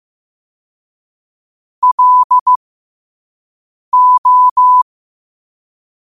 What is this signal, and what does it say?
Morse code, heard as a keyed tone, about 15 wpm, spaced as for 5 wpm, 1000 Hz. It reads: LO
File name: qotdfiles / 5wpm